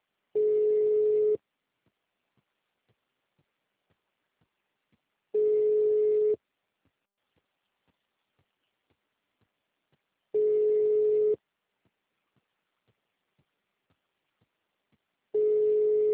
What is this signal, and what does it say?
Ringback tone (Europe, 425 Hz).